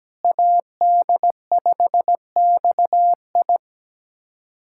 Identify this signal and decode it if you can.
Morse code, heard as a keyed tone, about 17 wpm, 700 Hz. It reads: AD5XI
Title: K1USN SST QSO Practice - 17wpm